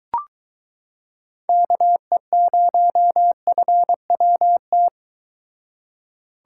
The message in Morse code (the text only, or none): KE0FWT